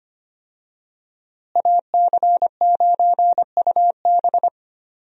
Morse code, heard as a keyed tone, about 25 wpm, 700 Hz. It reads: AC9UB
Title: USA Call Signs - 5 Characters - Repeated 2x - 25wpm